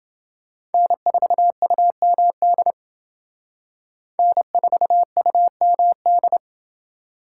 Morse code, heard as a keyed tone, about 30 wpm, 700 Hz. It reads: N4UMD N4UMD